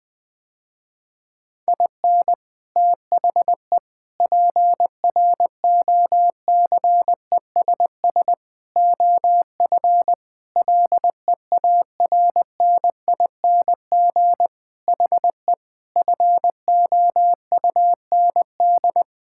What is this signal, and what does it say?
Morse code, heard as a keyed tone, about 21 wpm, 700 Hz. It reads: IN THE PROCESS OF LEARNING HE FOUND
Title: Sentences with 4-Word Phrases - Spoken Beginning - 20wpm